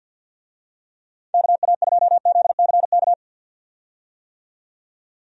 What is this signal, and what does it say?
Morse code, heard as a keyed tone, about 49 wpm, 700 Hz. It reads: KA2ZCK
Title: USA Call Signs - 6 Characters - 50wpm